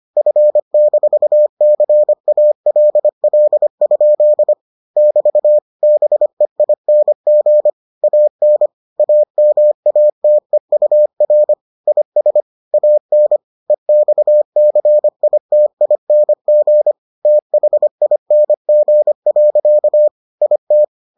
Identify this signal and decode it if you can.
Morse code, heard as a keyed tone, about 25 wpm, 600 Hz. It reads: F-CALL? <BT> BEING AN AMATEUR IS AN EXCITING THING. IT